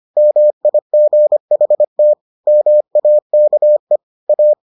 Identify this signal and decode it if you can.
Morse code, heard as a keyed tone, about 25 wpm, 600 Hz. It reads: MIGHT MAKE A